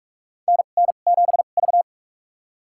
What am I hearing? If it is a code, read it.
Morse code, heard as a keyed tone, about 41 wpm, spaced as for 31 wpm, 700 Hz. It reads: NN7V
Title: USA Call Signs - 4 Characters - 2x Character Spacing - 45wpm